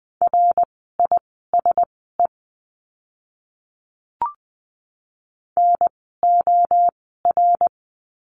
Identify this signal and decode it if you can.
Morse code, heard as a keyed tone, about 20 wpm, spaced as for 15 wpm, 700 Hz. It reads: RISE NOR
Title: 054 Lesson 10f - Letter r - Words 20wpm